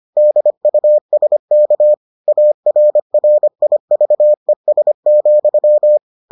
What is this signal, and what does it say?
Morse code, heard as a keyed tone, about 25 wpm, 600 Hz. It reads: DUSK ARRIVES,